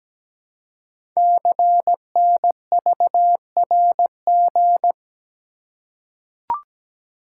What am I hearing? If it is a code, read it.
Morse code, heard as a keyed tone, about 17 wpm, 700 Hz. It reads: CNVRG
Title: Code Groups - 5 Letters - 17wpm